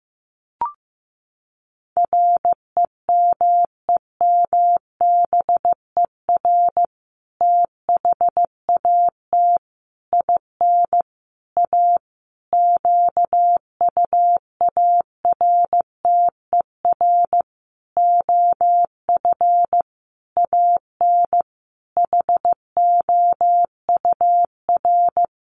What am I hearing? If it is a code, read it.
Morse code, heard as a keyed tone, about 15 wpm, 700 Hz. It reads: REMEMBER THAT IN A QUARTER OF AN HOUR
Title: Sentences from Top 1300 Words 15wpm